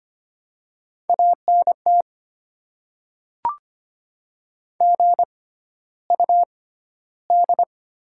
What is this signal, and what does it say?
Morse code, heard as a keyed tone, about 25 wpm, spaced as for 13 wpm, 700 Hz. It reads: ANT GUD
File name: Top 20 QSO Elements - 6x Character Spacing - 25wpm